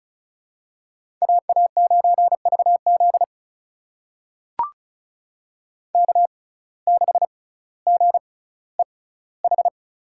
Morse code, heard as a keyed tone, about 34 wpm, spaced as for 18 wpm, 700 Hz. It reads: AA9VZ K6GEH
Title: USA Call Signs - 5 Characters - 6x Character Spacing - 35wpm